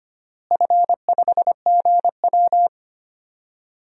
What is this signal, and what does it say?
Morse code, heard as a keyed tone, about 25 wpm, 700 Hz. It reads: F5GW
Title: DXCC-100 Call Signs - 4 Characters - Repeated 3x - 25wpm